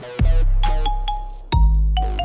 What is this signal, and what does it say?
There is no code.